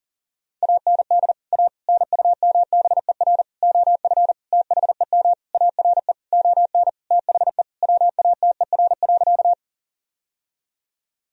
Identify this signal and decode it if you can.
Morse code, heard as a keyed tone, about 40 wpm, 700 Hz. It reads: AND A NUMBER OF THEM ARE ON THE WATER.
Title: Sentences from Top 1000 Words - Review Part 1 - 40wpm